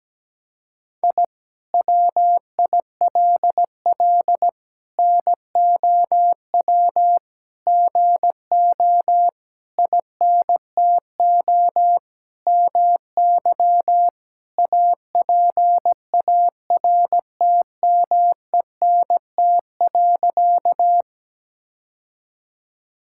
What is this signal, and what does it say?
Morse code, heard as a keyed tone, about 17 wpm, 700 Hz. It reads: I WILL NOW GO INTO MY APARTMENT.
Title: Sentences from Top 1000 Words - Review Part 6 - 17wpm